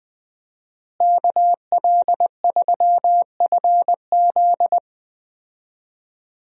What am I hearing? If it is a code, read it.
Morse code, heard as a keyed tone, about 20 wpm, 700 Hz. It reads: KL3FZ
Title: USA Call Signs - 5 Characters - Repeated 3x - 20wpm